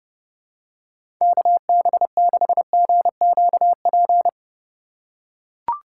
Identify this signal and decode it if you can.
Morse code, heard as a keyed tone, about 32 wpm, 700 Hz. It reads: KB6GQP
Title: USA Call Signs - 6 Characters - 6x Character Spacing - 30wpm